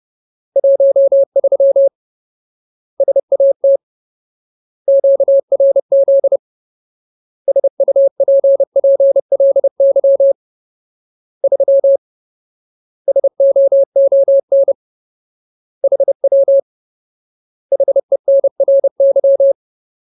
Morse code, heard as a keyed tone, about 30 wpm, 550 Hz. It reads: 13 SAT QRZ SUPPLY 3 SOON HW HENRY
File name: Ultimate Word List / 30WPM Ultimate